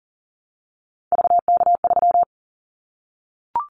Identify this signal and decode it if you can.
Morse code, heard as a keyed tone, about 39 wpm, 700 Hz. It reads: VK3
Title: Callsign Prefixes for Top 50 DXCC - 40wpm